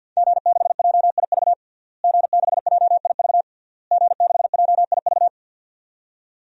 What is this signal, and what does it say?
Morse code, heard as a keyed tone, about 52 wpm, 700 Hz. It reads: G6JIV G6JIV G6JIV
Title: DXCC-100 Call Signs - 5 Characters - Repeated 3x - 50wpm